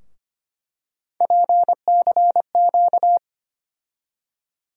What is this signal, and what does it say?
Morse code, heard as a keyed tone, about 25 wpm, 700 Hz. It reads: PCQ
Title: USA Call Sign Postfixes - 3 Letters 25wpm